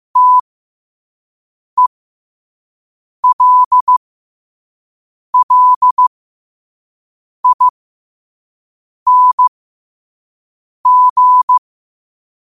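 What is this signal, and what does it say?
Morse code, heard as a keyed tone, about 15 wpm, spaced as for 5 wpm, 1000 Hz. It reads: TELLING
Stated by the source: Morse practice quote sent at 5 wpm.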